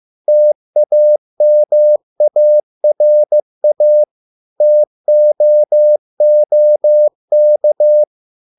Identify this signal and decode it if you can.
Morse code, heard as a keyed tone, about 15 wpm, 600 Hz. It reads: TAMARA TOOK